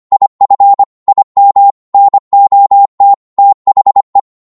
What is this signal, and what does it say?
Morse code, heard as a keyed tone, about 25 wpm, 800 Hz. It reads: IF IM NOT THE